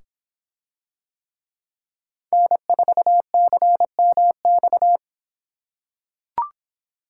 Morse code, heard as a keyed tone, about 24 wpm, 700 Hz. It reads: N4CMX